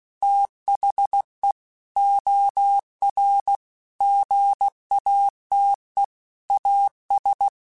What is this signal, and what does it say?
Morse code, heard as a keyed tone, about 16 wpm, 800 Hz. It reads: THE OR GATE AS
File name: Technical Subjects / Technical Subjects-mp3s / OR Gate @15